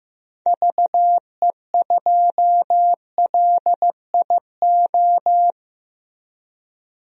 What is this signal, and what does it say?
Morse code, heard as a keyed tone, about 15 wpm, 700 Hz. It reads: VE2LIO
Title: DXCC-100 Call Signs - 6 Characters - Repeated 2x - 15wpm